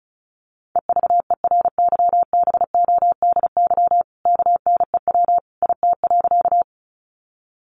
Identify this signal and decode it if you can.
Morse code, heard as a keyed tone, about 34 wpm, 700 Hz. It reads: EVERYBODY KNEW IT.
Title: Sentences from Top 1400 Words 35wpm